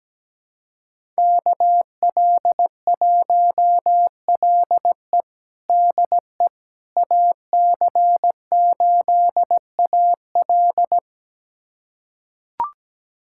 Morse code, heard as a keyed tone, about 17 wpm, 700 Hz. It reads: KL1LE DE AC8AL
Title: USA Callsign DE Callsign - 5 Characters - 17wpm